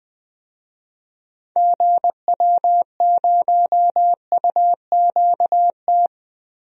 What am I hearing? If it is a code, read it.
Morse code, heard as a keyed tone, about 21 wpm, 700 Hz. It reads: GW0UQT
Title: DXCC-100 Call Signs - 6 Characters - Repeated 2x - 20wpm